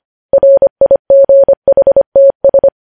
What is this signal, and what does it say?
Morse code, heard as a keyed tone, about 25 wpm, 550 Hz. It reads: RIGHTS
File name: podcast / News headlines-1x-25x25